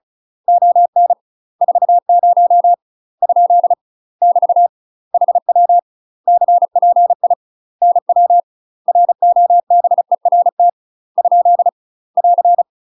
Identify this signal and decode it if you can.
Morse code, heard as a keyed tone, about 34 wpm, 700 Hz. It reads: ON 40 ? <BT> HW CPI NW ROBERT ? <AR>
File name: UK MASTER QSOs TRAINING / 35WPM / UK MASTER QSOs 0005